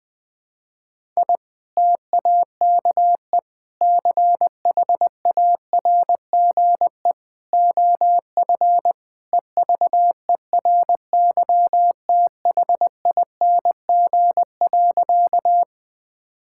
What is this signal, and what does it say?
Morse code, heard as a keyed tone, about 19 wpm, 700 Hz. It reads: I TAKE CHARGE OF EVERYTHING.
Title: Sentences from Top 1000 Words - Review Part 6 - 20wpm